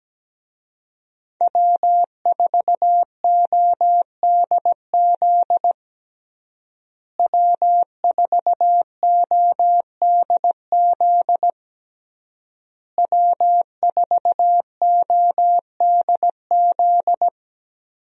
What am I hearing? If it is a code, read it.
Morse code, heard as a keyed tone, about 17 wpm, 700 Hz. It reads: W4ODZ W4ODZ W4ODZ
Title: USA Call Signs - 5 Characters - Repeated 3x - 17wpm